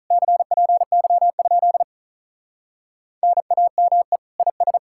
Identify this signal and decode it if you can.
Morse code, heard as a keyed tone, about 46 wpm, 700 Hz. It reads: CPY? NAME IS